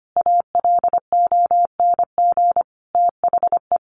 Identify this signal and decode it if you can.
Morse code, heard as a keyed tone, about 25 wpm, 700 Hz. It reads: ALONG THE
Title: Sentences from Top 1100 Words 25wpm